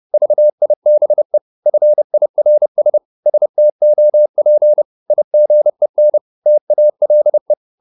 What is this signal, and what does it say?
Morse code, heard as a keyed tone, about 30 wpm, 600 Hz. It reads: VIBE FIRS STOP IGEN TALE
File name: Korte ord DK 30wpm